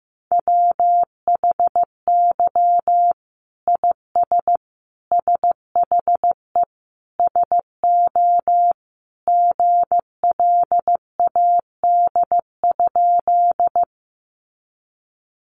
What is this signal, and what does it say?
Morse code, heard as a keyed tone, about 15 wpm, 700 Hz. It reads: WHY IS SHE SO GLAD?
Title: Sentences from Top 800 Words 15wpm